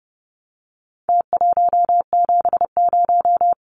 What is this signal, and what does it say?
Morse code, heard as a keyed tone, about 30 wpm, 700 Hz. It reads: T170
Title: Numbers - Contest Style - 4-Digit 30wpm